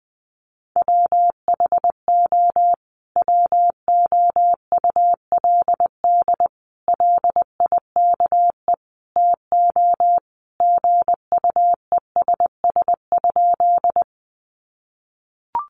Morse code, heard as a keyed tone, about 20 wpm, 700 Hz. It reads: WHO WOULD LIKE TO GUESS?
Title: Sentences from Top 1500 Words 20wpm